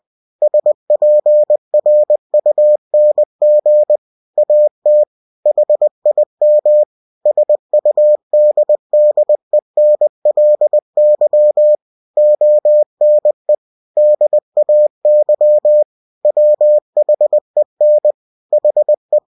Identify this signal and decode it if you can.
Morse code, heard as a keyed tone, about 20 wpm, 600 Hz. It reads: SPRUNG AT HIM SUDDENLY ONE DAY WHEN HE
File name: winnie-20wpm